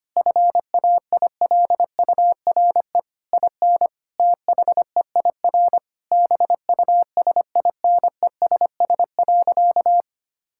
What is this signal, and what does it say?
Morse code, heard as a keyed tone, about 25 wpm, 700 Hz. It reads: FAILURE IN THEIR BUSINESS.
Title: Sentences from Top 1000 Words - Review Part 6 - 25wpm